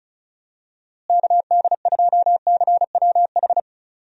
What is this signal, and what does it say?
Morse code, heard as a keyed tone, about 34 wpm, 700 Hz. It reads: KD2CWH
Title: USA Call Signs - 6 Characters - Repeated 3x - 35wpm